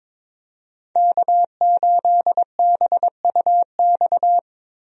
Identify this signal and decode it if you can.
Morse code, heard as a keyed tone, about 22 wpm, 700 Hz. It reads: K8BUX
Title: USA Call Signs - 5 Characters - Repeated 3x - 22wpm